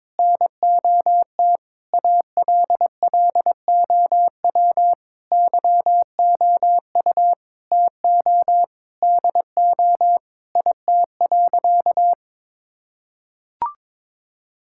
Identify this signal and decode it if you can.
Morse code, heard as a keyed tone, about 22 wpm, 700 Hz. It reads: NOT ALLOW YOU TO DO IT.
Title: Sentences from Top 1000 Words - Review Part 5 - 22wpm